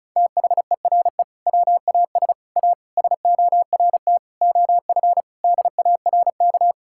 Morse code, heard as a keyed tone, about 36 wpm, 700 Hz. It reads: THERE WAS A SORT OF DARK
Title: Sentences from Top 1600 Words 35wpm